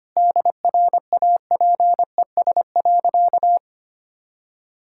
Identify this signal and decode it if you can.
Morse code, heard as a keyed tone, about 25 wpm, 700 Hz. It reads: DRAPES.
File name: Easy Sentences with Hints - 25wpm